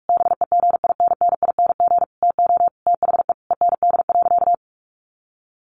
Morse code, heard as a keyed tone, about 44 wpm, 700 Hz. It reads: BEGINNING TO THE END.
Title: Sentences from Top 1300 Words 45wpm